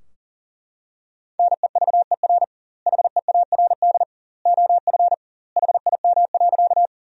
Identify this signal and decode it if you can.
Morse code, heard as a keyed tone, about 40 wpm, 700 Hz. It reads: NEVER HEARD OF HIM.
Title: 2x Word-Spacing - Sentences from Top 400 Words - 40wpm